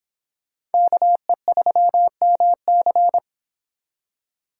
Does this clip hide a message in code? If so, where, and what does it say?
Yes: Morse code from 0.7 to 3.2 s, heard as a keyed tone, about 30 wpm, 700 Hz. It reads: KE3MC